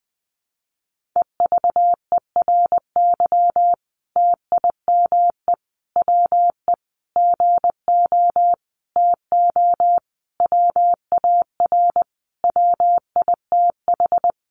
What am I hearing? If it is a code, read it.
Morse code, heard as a keyed tone, about 20 wpm, 700 Hz. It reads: EVERY TIME WE GO TO WAR WITH
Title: Sentences from Top 1300 Words 20wpm